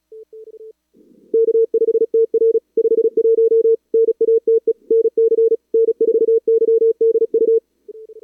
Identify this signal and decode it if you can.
Morse code, heard as a keyed tone, about 37 wpm, 450 Hz. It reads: K5TR 51 NATE NC N4YDU
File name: CWSprint / Feb2022 / k5tr-feb-na-sprint-cw-k5ot -0100